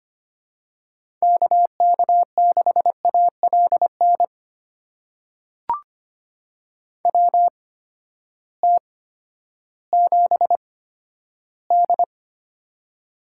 Morse code, heard as a keyed tone, about 25 wpm, spaced as for 12 wpm, 700 Hz. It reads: KK6ALN WT7D
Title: USA Call Signs - 8x Character Spacing - 25wpm